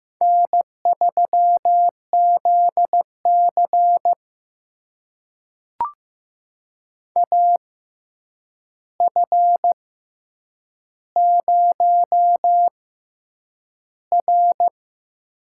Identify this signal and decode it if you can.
Morse code, heard as a keyed tone, about 15 wpm, spaced as for 8 wpm, 700 Hz. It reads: N3ZC AF0R
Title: USA Call Signs - 4 Characters - 6x Character Spacing - 15wpm